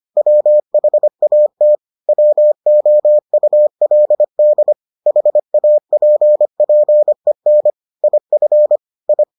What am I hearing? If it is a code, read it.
Morse code, heard as a keyed tone, about 25 wpm, 600 Hz. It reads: WHAT WOULD HAPPEN IF I